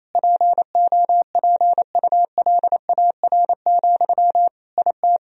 Morse code, heard as a keyed tone, about 27 wpm, 700 Hz. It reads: POPULAR, IT
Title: Sentences from Top 1000 Words - Review Part 3 - 28wpm